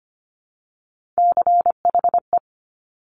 Morse code, heard as a keyed tone, about 25 wpm, 700 Hz. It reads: CHE
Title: Top Three Letter Combinations in Words 25wpm